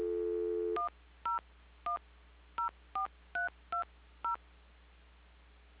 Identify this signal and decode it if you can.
DTMF, heard as telephone touch tones, tones of about 120 ms. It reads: 10104320
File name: recordings / PortlandUnion Millennium